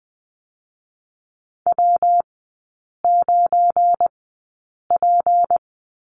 Morse code, heard as a keyed tone, about 20 wpm, spaced as for 8 wpm, 690 Hz. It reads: W9P
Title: 205 Lesson 33e - Letter j - Three Characters 20wpm